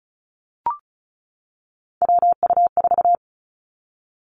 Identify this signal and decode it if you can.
Morse code, heard as a keyed tone, about 36 wpm, 700 Hz. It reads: WU4
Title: Three Challenging Letter-Number Combinations 35wpm